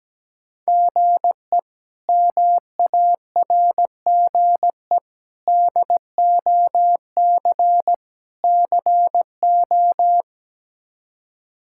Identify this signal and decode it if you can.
Morse code, heard as a keyed tone, about 17 wpm, 700 Hz. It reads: GE MARGE DOC CO
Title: K1USN SST QSO Practice - 17wpm